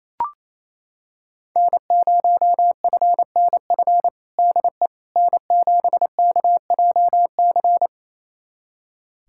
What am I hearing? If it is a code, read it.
Morse code, heard as a keyed tone, about 28 wpm, 700 Hz. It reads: N0FNF DE N7KJC
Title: USA Callsign DE Callsign - 5 Characters - 28wpm